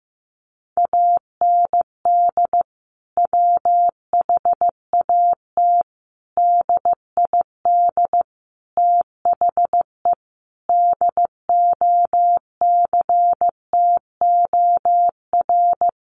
Morse code, heard as a keyed tone, about 15 wpm, 700 Hz. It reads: AND WHAT DID THE DOCTOR
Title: Sentences from Top 1500 Words 15wpm